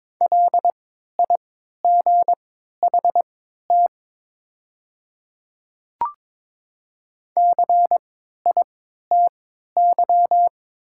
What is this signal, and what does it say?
Morse code, heard as a keyed tone, about 22 wpm, spaced as for 13 wpm, 700 Hz. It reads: LIGHT CITY
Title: 3x Character Spacing - Top 300 Words - 22wpm